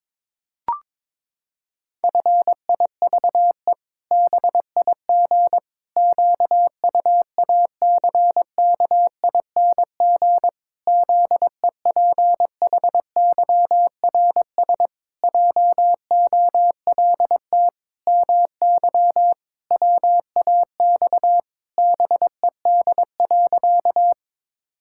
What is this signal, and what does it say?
Morse code, heard as a keyed tone, about 22 wpm, 700 Hz. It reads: FIVE BIG QUACKING ZEPHYRS JOLT MY WAX BED.
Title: Pangram Challenge v2 - 22wpm